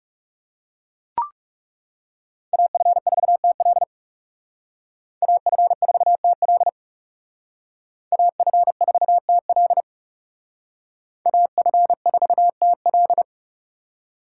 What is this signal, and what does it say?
Morse code, heard as a keyed tone, about 36 wpm, 700 Hz. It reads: AF4TL AF4TL AF4TL AF4TL
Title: Speed Racing - USA 2x2 Call Signs 30wpm